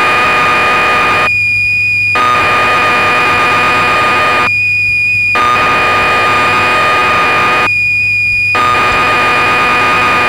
Cassette data (Kansas City type).
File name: magazines / Acorn User / 1984-09 BBC Programs